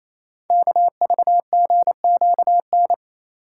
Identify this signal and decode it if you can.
Morse code, heard as a keyed tone, about 29 wpm, 700 Hz. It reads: KVGQN
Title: Code Groups - 5 Letters - 28wpm